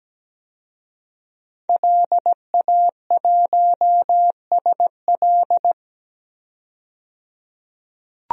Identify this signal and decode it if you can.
Morse code, heard as a keyed tone, about 17 wpm, 700 Hz. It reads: LA1SL
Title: DXCC-100 Call Signs - 5 Characters - 17wpm